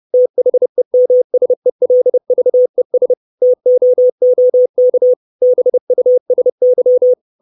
Morse code, heard as a keyed tone, about 30 wpm, 500 Hz. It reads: THEMSELVES TOOK BUSY
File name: 632 Words 30WPM